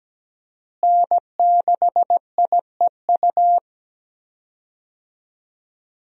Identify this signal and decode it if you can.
Morse code, heard as a keyed tone, about 17 wpm, 700 Hz. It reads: N6IEU
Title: USA Call Signs - 5 Characters - Repeated 2x - 17wpm